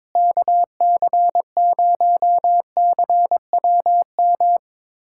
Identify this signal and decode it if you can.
Morse code, heard as a keyed tone, about 22 wpm, 700 Hz. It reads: KC0CWM
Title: USA Callsign DE Callsign 22wpm